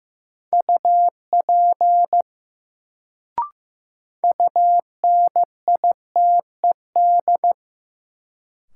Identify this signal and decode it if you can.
Morse code, heard as a keyed tone, about 15 wpm, 700 Hz. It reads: UP UNITED
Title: Sets of 2 Words - New Word Drill - 15wpm